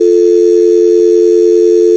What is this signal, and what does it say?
Dial tone (North America), 350 + 440 Hz.